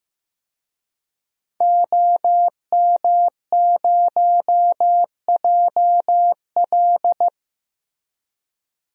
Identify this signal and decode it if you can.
Morse code, heard as a keyed tone, about 15 wpm, 700 Hz. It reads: OM0JL
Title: DXCC-100 Call Signs - 5 Characters - Repeated 3x - 15wpm